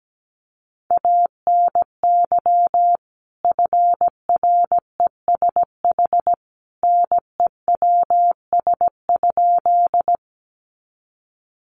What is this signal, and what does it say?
Morse code, heard as a keyed tone, about 17 wpm, 700 Hz. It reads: ANY FRESH NEWS?
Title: Sentences from Top 1200 Words 17wpm